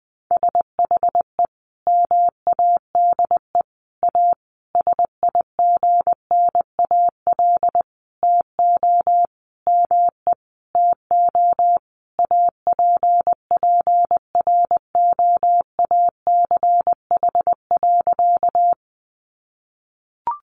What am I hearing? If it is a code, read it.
Morse code, heard as a keyed tone, about 19 wpm, 700 Hz. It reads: SHE MADE A SIGNAL TO ME TO APPROACH.
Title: Sentences from Top 1400 Words 20wpm